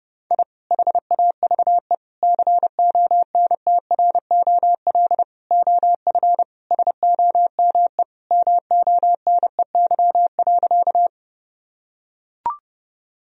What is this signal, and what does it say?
Morse code, heard as a keyed tone, about 32 wpm, 700 Hz. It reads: I HAVE CONTROL OF SOME MONEY.
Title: Sentences from Top 1000 Words - Review Part 3 - 30wpm